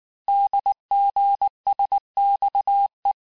Morse code, heard as a keyed tone, about 19 wpm, 800 Hz. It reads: DGSXE
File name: cours-du-mercredi / audios / cours-869-texte-CHIFFRE-vitesse-18-MPM